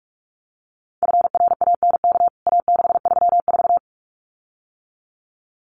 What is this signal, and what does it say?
Morse code, heard as a keyed tone, about 44 wpm, 700 Hz. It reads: FRANK A634
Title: CWT Exchange 45wpm